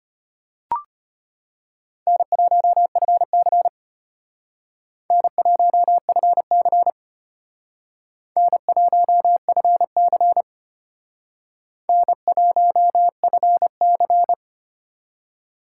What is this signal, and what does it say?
Morse code, heard as a keyed tone, about 32 wpm, 700 Hz. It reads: N1FC N1FC N1FC N1FC